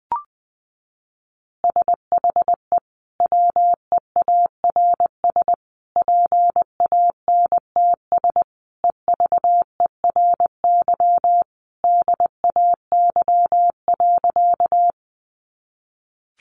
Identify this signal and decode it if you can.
Morse code, heard as a keyed tone, about 20 wpm, 700 Hz. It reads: SHE WEARS PANTS EVERY DAY.
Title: Additional Easy Phrases 20wpm